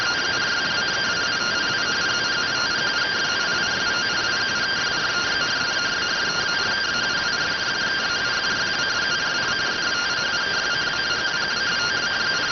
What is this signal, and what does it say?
Two-tone teleprinter signal, about 170 Hz shift.